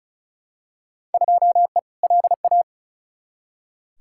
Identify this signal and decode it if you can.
Morse code, heard as a keyed tone, about 36 wpm, 700 Hz. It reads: 2E LA
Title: Field Day Exchanges Encore - 35wpm